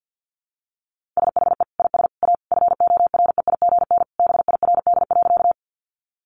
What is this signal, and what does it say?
Morse code, heard as a keyed tone, about 50 wpm, 700 Hz. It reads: SHE IS A FOREIGN BIRD.
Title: Sentences from Top 1400 Words 50wpm